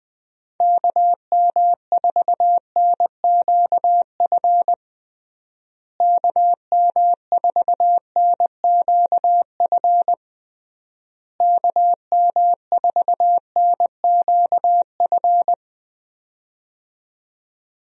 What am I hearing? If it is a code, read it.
Morse code, heard as a keyed tone, about 21 wpm, 700 Hz. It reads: KM4NQF KM4NQF KM4NQF
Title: USA Call Signs - 6 Characters - Repeated 3x - 20wpm